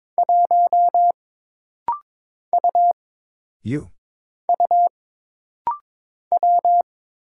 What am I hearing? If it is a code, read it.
Morse code, heard as a keyed tone, about 22 wpm, 700 Hz. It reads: 1 U U W
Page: This rapid-fire timing is geared to push your ICR (Instant Character Recognition) skills.